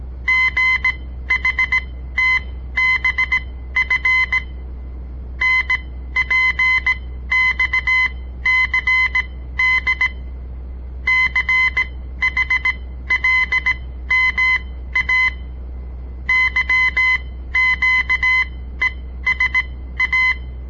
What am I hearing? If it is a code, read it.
Morse code, heard as a keyed tone, about 18 wpm, spaced as for 13 wpm, 2050 Hz. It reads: GHTBF NPXCD CHLMA YQESA
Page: On the air Morse code practice from the Southern Montana Amatuer Radio Association; all code, no read back